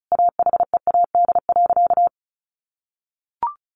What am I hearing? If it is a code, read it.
Morse code, heard as a keyed tone, about 34 wpm, 700 Hz. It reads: AHEAD.
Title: Sentences from Top 1100 Words 35wpm